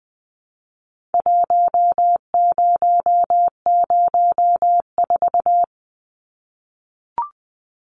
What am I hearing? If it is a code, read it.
Morse code, heard as a keyed tone, about 20 wpm, 700 Hz. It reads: 1004
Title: Numbers - 4-Digit 20wpm